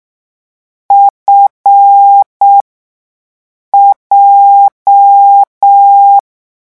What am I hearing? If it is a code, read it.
Morse code, heard as a keyed tone, about 6 wpm, spaced as for 5 wpm, 800 Hz. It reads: FJ
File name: morse mp3 / lecon7-vitesse 6